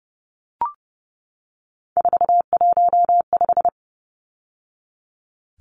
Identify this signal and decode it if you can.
Morse code, heard as a keyed tone, about 30 wpm, 700 Hz. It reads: 415
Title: Numbers - 3-Digit 30wpm